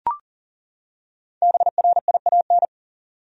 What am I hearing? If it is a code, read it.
Morse code, heard as a keyed tone, about 40 wpm, 700 Hz. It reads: BRIAN
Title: USA Common Names - Max 5 Letters - 40wpm